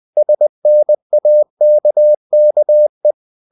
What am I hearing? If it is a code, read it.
Morse code, heard as a keyed tone, about 20 wpm, 600 Hz. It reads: SNAKKE
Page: Ferie 20 wpm